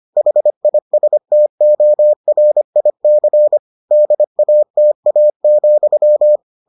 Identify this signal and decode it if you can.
Morse code, heard as a keyed tone, about 25 wpm, 600 Hz. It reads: HISTORIC DATA,